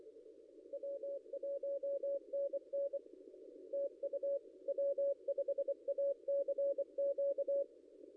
Morse code, heard as a keyed tone, about 25 wpm, 560 Hz. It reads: W1NN TU W5ACQ
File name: contests / ss-cw-k5tr-op-n5rz -0700